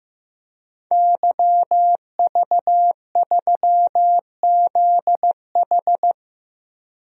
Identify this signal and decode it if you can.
Morse code, heard as a keyed tone, about 15 wpm, 700 Hz. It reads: YV3ZH
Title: DXCC-100 Call Signs - 5 Characters - 15wpm